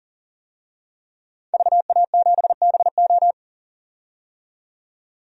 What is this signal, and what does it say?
Morse code, heard as a keyed tone, about 40 wpm, 700 Hz. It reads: VA7BO
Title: DXCC-100 Call Signs - 5 Characters - 40wpm